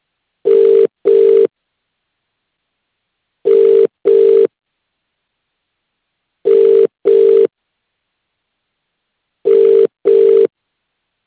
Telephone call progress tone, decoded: ringback tone